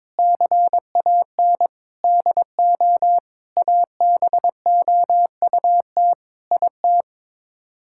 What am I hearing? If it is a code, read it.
Morse code, heard as a keyed tone, about 22 wpm, 700 Hz. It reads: CAN DO ABOUT IT
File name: practice / 22wpm / Sets of 8 Words - 22wpm